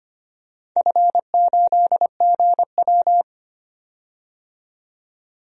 Morse code, heard as a keyed tone, about 25 wpm, 700 Hz. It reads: F8GW
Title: DXCC-100 Call Signs - 4 Characters - 25wpm